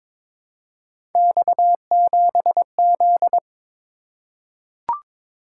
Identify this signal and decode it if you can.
Morse code, heard as a keyed tone, about 22 wpm, 700 Hz. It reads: X7Z
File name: practice / 22wpm / Three Letter-Number Combinations - 22wpm